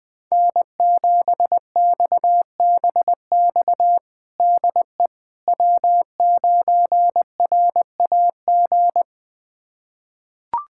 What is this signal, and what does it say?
Morse code, heard as a keyed tone, about 20 wpm, 700 Hz. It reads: N7XBX DE W9RAG
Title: USA Callsign DE Callsign - 5 Characters - 20wpm